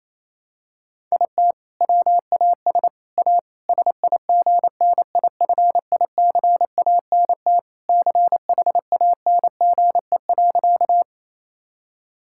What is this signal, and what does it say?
Morse code, heard as a keyed tone, about 28 wpm, 700 Hz. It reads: IT WAS A SIGNIFICANT CHANGE.
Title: Sentences from Top 1000 Words - Review Part 6 - 28wpm